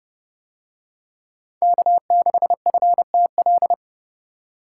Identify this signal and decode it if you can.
Morse code, heard as a keyed tone, about 29 wpm, 700 Hz. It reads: K6FTL